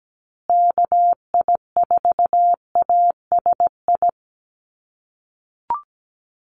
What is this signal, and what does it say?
Morse code, heard as a keyed tone, about 17 wpm, 700 Hz. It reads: KI4ASI
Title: USA Call Signs Repeated 3x 17wpm